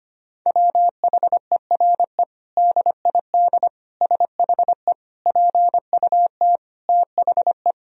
Morse code, heard as a keyed tone, about 25 wpm, 700 Hz. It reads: WHERE DID SHE PUT THE
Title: Sentences from Top 1000 Words - Review Part 3 - 25wpm